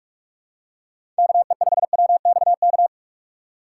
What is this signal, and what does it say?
Morse code, heard as a keyed tone, about 44 wpm, 700 Hz. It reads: KE5WXK
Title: USA Call Signs - 6 Characters - 6x Character Spacing - 45wpm